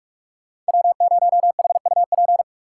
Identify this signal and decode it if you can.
Morse code, heard as a keyed tone, about 42 wpm, 690 Hz. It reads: W0HUP
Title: USA Call Signs - 5 Characters - 6x Character Spacing - 45wpm